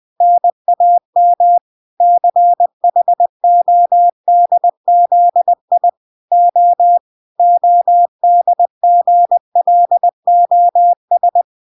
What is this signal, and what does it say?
Morse code, heard as a keyed tone, about 20 wpm, 700 Hz. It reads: NAM CHODZI O ODGLOS